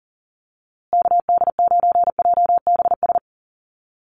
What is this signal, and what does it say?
Morse code, heard as a keyed tone, about 40 wpm, 700 Hz. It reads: KD9JBS
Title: USA Call Signs Repeated 3x 40wpm